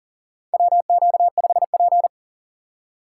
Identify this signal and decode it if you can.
Morse code, heard as a keyed tone, about 40 wpm, 700 Hz. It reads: WQ5P
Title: USA Call Signs - 6x Character Spacing - 40wpm